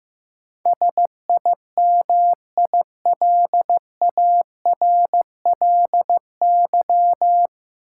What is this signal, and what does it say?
Morse code, heard as a keyed tone, about 15 wpm, 700 Hz. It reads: SIMILARLY
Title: Sentences from Top 1400 Words - New Word Drill Alt - 15wpm